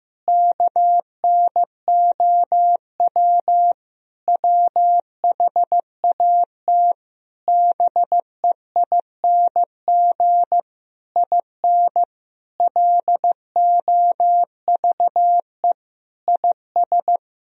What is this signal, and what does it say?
Morse code, heard as a keyed tone, about 16 wpm, 700 Hz. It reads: KNOW WHAT BEING IN LOVE IS
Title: Sentences from Top 1000 Words - Review Part 3 - 15wpm